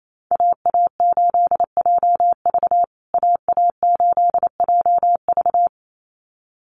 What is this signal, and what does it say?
Morse code, heard as a keyed tone, about 28 wpm, 700 Hz. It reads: AA8JV AA8JV
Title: USA Call Signs Repeated 2x 28wpm